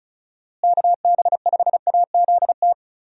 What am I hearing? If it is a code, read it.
Morse code, heard as a keyed tone, about 34 wpm, 700 Hz. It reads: KB5AZT
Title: USA Call Signs - 6 Characters - Repeated 3x - 35wpm